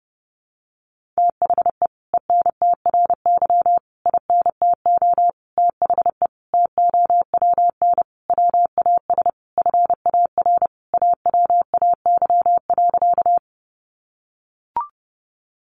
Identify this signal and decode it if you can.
Morse code, heard as a keyed tone, about 30 wpm, 700 Hz. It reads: THE ENTRY INTO THE TOWN WAS FAR AWAY.
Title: Sentences from Top 1500 Words 30wpm